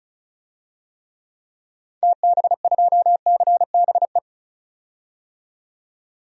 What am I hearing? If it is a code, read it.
Morse code, heard as a keyed tone, about 34 wpm, 700 Hz. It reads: TB2CBE
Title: DXCC-100 Call Signs - 6 Characters - 35wpm